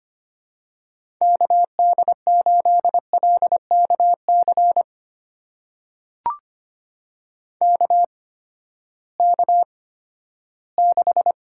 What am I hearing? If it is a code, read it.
Morse code, heard as a keyed tone, about 25 wpm, spaced as for 14 wpm, 700 Hz. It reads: KD8LKC KK6
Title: USA Call Signs - 6 Characters - 8x Character Spacing - 25wpm